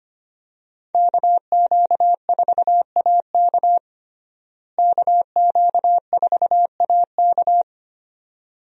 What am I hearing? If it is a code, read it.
Morse code, heard as a keyed tone, about 25 wpm, 700 Hz. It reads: KQ4AK KQ4AK
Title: USA Call Signs - 5 Characters - Repeated 2x - 25wpm